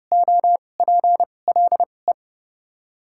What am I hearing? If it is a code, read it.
Morse code, heard as a keyed tone, about 30 wpm, spaced as for 22 wpm, 700 Hz. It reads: OPLE
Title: 2x Character Spacing - Top 100 Words - 30wpm